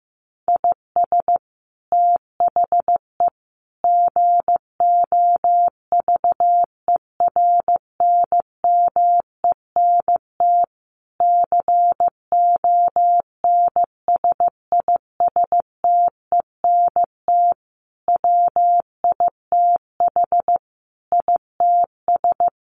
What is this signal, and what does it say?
Morse code, heard as a keyed tone, about 14 wpm, 700 Hz. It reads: IS THE GOVERNMENT CONSISTENT WITH ITS
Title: Sentences from Top 1500 Words 15wpm